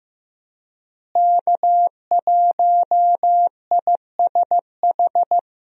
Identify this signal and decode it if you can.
Morse code, heard as a keyed tone, about 15 wpm, 700 Hz. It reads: K1ISH
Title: USA Call Signs - 5 Characters - Repeated 3x - 15wpm